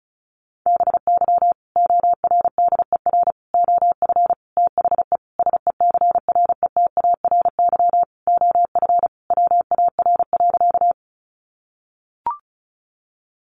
Morse code, heard as a keyed tone, about 34 wpm, 700 Hz. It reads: BY ORDER OF THE SECRETARY OF WAR.
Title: Sentences from Top 1300 Words 35wpm